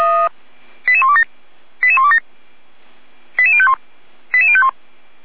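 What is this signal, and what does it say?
Five-tone selective call (ZVEI), heard as quick runs of five tones, tones of about 70 ms. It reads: 88127 88127 88051 88051
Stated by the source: ZVEI five-tone selective call received by SDR, from the Feuerwehr Orlamünde audio server.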